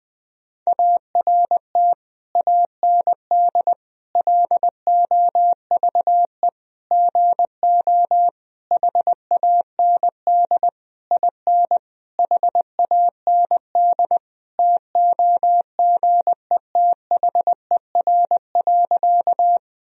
Morse code, heard as a keyed tone, about 21 wpm, 700 Hz. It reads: ART AND LOVE GO HAND IN HAND TOGETHER.
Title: Sentences from Top 1000 Words - Review Part 3 - 20wpm